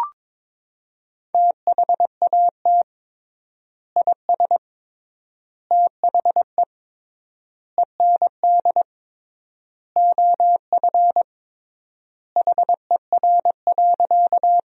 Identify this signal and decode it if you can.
Morse code, heard as a keyed tone, about 22 wpm, 700 Hz. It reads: THAT IS THE END OF HER.
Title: 3x Word-Spacing - Sentences from Top 200 Words - 22wpm